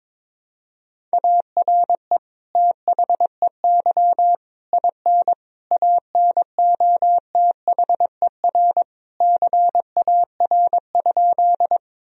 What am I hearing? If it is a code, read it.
Morse code, heard as a keyed tone, about 22 wpm, 700 Hz. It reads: ARE THEY IN ANOTHER CAR?
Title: Sentences from Top 1000 Words - Review Part 1 - 22wpm